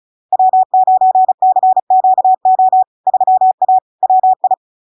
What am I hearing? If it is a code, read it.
Morse code, heard as a keyed tone, about 36 wpm, 750 Hz. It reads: W9CQO 3A WI
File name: LARK-FD-2022-Log-in-CW-35wpm-750-Hz